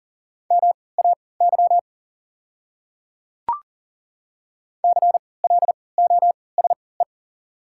Morse code, heard as a keyed tone, about 40 wpm, spaced as for 23 wpm, 690 Hz. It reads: MAY CLOSE
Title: 3x Character Spacing - Top 300 Words - 40wpm